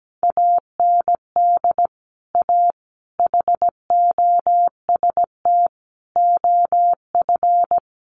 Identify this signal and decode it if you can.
Morse code, heard as a keyed tone, about 17 wpm, 700 Hz. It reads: AND A HOST OF
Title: Sets of 5 Words Encore2 17wpm